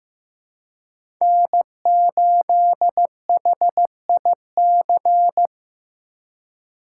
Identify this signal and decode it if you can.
Morse code, heard as a keyed tone, about 15 wpm, 700 Hz. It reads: N8HIC
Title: USA Call Signs - 5 Characters - 3x Character Spacing - 15wpm